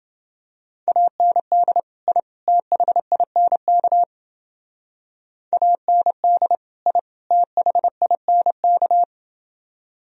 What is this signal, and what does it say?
Morse code, heard as a keyed tone, about 29 wpm, 700 Hz. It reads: AND I THINK AND I THINK